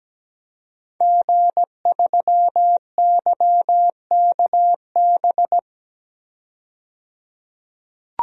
Morse code, heard as a keyed tone, about 17 wpm, 700 Hz. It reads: G3YKB
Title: DXCC-100 Call Signs - 5 Characters - 17wpm